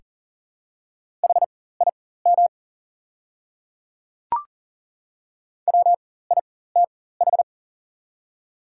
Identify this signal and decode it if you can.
Morse code, heard as a keyed tone, about 40 wpm, spaced as for 19 wpm, 690 Hz. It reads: HIM WITH
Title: 4x Character Spacing - Top 100 Words - 40wpm